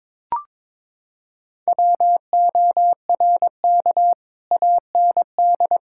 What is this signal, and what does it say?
Morse code, heard as a keyed tone, about 22 wpm, 700 Hz. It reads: WORK AND
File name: practice / 22wpm / Sets of 8 Words - 22wpm